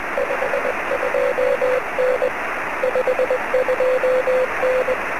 Morse code, heard as a keyed tone, about 20 wpm, 510 Hz, with radio noise. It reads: 52N 52N
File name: sm7lcb / 10 oz1ff 23cm